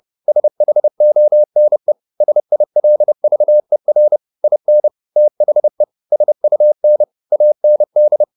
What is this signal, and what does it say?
Morse code, heard as a keyed tone, about 30 wpm, 600 Hz. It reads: SHONE SILVER IN THE SUN AND
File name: the old man and the sea / the old man and the sea 1 30wpm